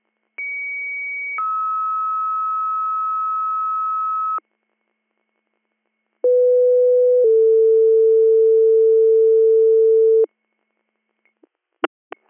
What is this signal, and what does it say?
Two-tone page, heard as a one-second tone, then a three-second tone, about 2250 then 1290 Hz, about 500 then 450 Hz. It is heard from 0.4 to 10.2 s.